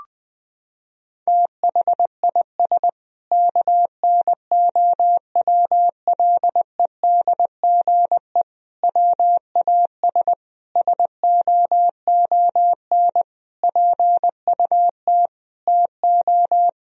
Morse code, heard as a keyed tone, about 20 wpm, 700 Hz. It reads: THIS KNOWLEDGE WAS SOON PUT TO
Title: Sentences from Top 1000 Words - Review Part 3 - 20wpm